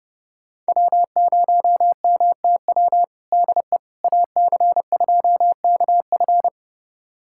Morse code, heard as a keyed tone, about 30 wpm, 700 Hz. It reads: W0MTW DE AC2KF
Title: USA Callsign DE Callsign - 5 Characters - 30wpm